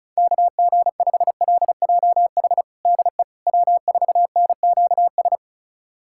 Morse code, heard as a keyed tone, about 36 wpm, 690 Hz. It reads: KG5LJH DE W4NQS
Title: USA Callsign DE Callsign 35wpm